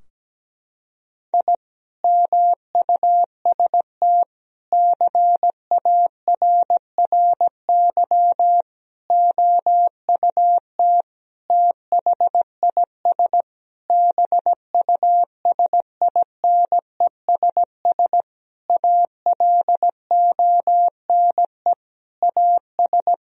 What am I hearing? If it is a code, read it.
Morse code, heard as a keyed tone, about 17 wpm, 700 Hz. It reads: I MUST CARRY OUT THIS BUSINESS ALONE AS
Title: Sentences from Top 1000 Words - Review Part 2 - 17wpm